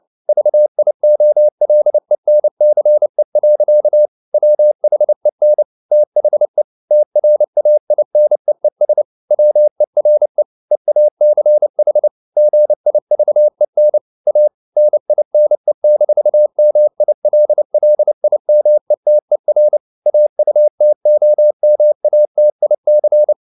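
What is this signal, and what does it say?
Morse code, heard as a keyed tone, about 29 wpm, 600 Hz. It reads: VIOLENCE. WHEN THE TRAINEES WERE EACH GIVEN A NINE-MILLIMETER AUTOMATIC
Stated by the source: Morse code at 29 wpm, sending Ken Follett's novel Never in English, from CW med Gnister.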